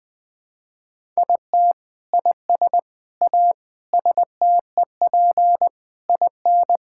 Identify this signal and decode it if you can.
Morse code, heard as a keyed tone, about 20 wpm, 700 Hz. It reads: IT IS A STEP IN
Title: Sentences with 8-Word Phrases - Spoken Beginning - 20wpm